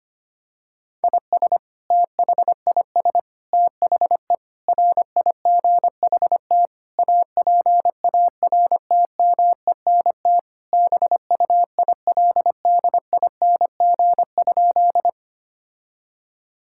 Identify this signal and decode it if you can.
Morse code, heard as a keyed tone, about 25 wpm, 700 Hz. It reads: IS THIS THE RIGHT APARTMENT BUILDING?
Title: Sentences from Top 1000 Words - Review Part 6 - 25wpm